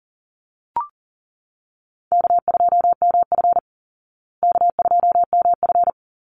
Morse code, heard as a keyed tone, about 40 wpm, 700 Hz. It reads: K2MF K2MF
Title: Easy Call Signs Repeated 2x - 40wpm